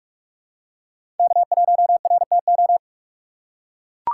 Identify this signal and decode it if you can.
Morse code, heard as a keyed tone, about 42 wpm, 700 Hz. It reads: K1RTO
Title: USA Call Signs - 5 Characters - 4x Character Spacing - 45wpm